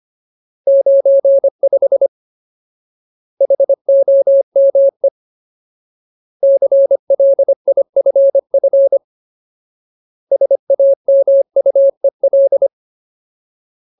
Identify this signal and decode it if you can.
Morse code, heard as a keyed tone, about 25 wpm, 550 Hz. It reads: 95 HOME CLIFF SAMUEL
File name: Ultimate Word List / 25WPM Ultimate